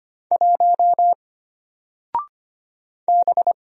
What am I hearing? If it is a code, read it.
Morse code, heard as a keyed tone, about 25 wpm, 700 Hz. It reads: 1 B
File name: Single-Letter-Number-25wpm